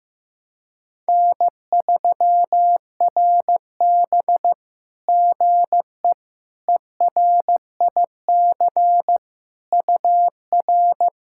Morse code, heard as a keyed tone, about 15 wpm, 700 Hz. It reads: N3RB GE ERIC UR
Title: SOTA QSO Practice 15wpm